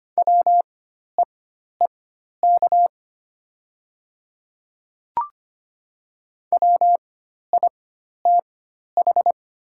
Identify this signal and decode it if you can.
Morse code, heard as a keyed tone, about 25 wpm, spaced as for 12 wpm, 700 Hz. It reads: WEEK WITH
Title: 4x Character Spacing - Top 500 Words - 25wpm